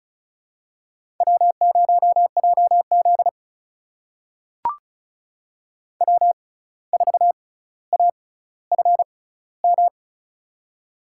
Morse code, heard as a keyed tone, about 36 wpm, spaced as for 17 wpm, 700 Hz. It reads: W0JZ W4AFM
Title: USA Call Signs - 6x Character Spacing - 35wpm